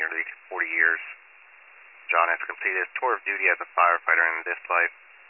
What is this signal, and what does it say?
Emergency Alert System (SAME) tones, heard as harsh buzzing bursts, with speech. It is heard from 0.5 to 4.6 s.